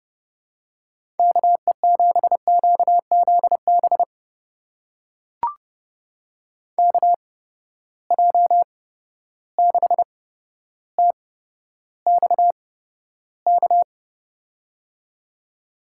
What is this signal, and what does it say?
Morse code, heard as a keyed tone, about 32 wpm, spaced as for 13 wpm, 700 Hz. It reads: KE7QZB KJ6TXK
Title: USA Call Signs - 6 Characters - 8x Character Spacing - 30wpm